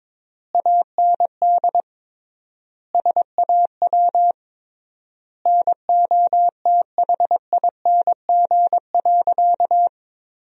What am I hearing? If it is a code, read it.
Morse code, heard as a keyed tone, about 22 wpm, 700 Hz. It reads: AND SAW NOTHING.
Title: 3x Word-Spacing - Sentences from Top 500 Words - 22wpm